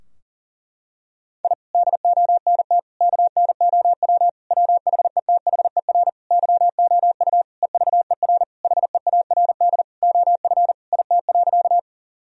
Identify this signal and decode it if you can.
Morse code, heard as a keyed tone, about 40 wpm, 700 Hz. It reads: I DONT KNOW WHETHER YOU EVER HEARD OF IT.
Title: Sentences from Top 1000 Words - Review Part 3 - 40wpm